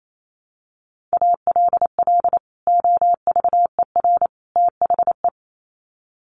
Morse code, heard as a keyed tone, about 28 wpm, 700 Hz. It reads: ALL OVER THE
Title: Sets of 3 Words 28wpm